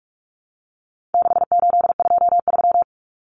Morse code, heard as a keyed tone, about 44 wpm, 700 Hz. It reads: 6823
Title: Numbers - 4-Digit 45wpm